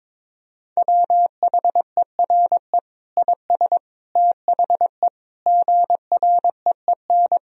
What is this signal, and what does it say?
Morse code, heard as a keyed tone, about 22 wpm, 700 Hz. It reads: WHERE IS THE GREEN
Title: Sentences from Top 1000 Words - Review Part 2 - 22wpm